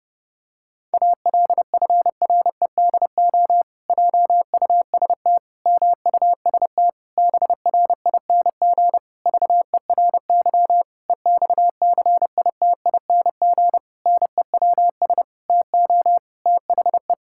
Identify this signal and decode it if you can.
Morse code, heard as a keyed tone, about 30 wpm, 700 Hz. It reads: ALFREDO JUST MUST BRING VERY EXCITING NEWS TO THE
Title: Pangram Challenge v2 - 30wpm